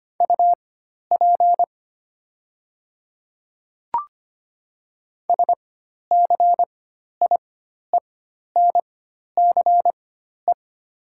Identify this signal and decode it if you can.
Morse code, heard as a keyed tone, about 25 wpm, spaced as for 12 wpm, 700 Hz. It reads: UP SCIENCE
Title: 4x Character Spacing - Top 500 Words - 25wpm